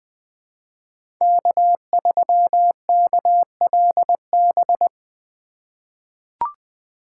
Morse code, heard as a keyed tone, about 20 wpm, 700 Hz. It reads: K3KLB
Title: 2023 POTA Activator Call Signs - 20wpm